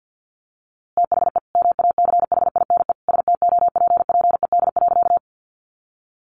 Morse code, heard as a keyed tone, about 50 wpm, 700 Hz. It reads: THE MACHINE STOPPED.
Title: A Sound Of Thunder 50wpm